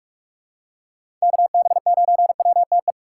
Morse code, heard as a keyed tone, about 45 wpm, 700 Hz. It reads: KB9WTE
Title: USA Call Signs - 6 Characters - 4x Character Spacing - 45wpm